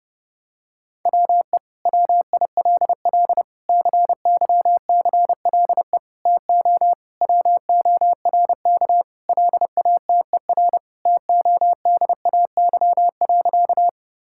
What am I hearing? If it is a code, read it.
Morse code, heard as a keyed tone, about 32 wpm, 700 Hz. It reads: WE WILL CYCLE TO WORK LATER TODAY.
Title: Sentences from Top 1000 Words - Review Part 6 - 30wpm